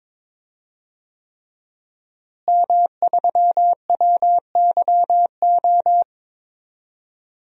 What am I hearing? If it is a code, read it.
Morse code, heard as a keyed tone, about 22 wpm, 700 Hz. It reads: M3WYO
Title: DXCC-100 Call Signs - 5 Characters - Repeated 3x - 22wpm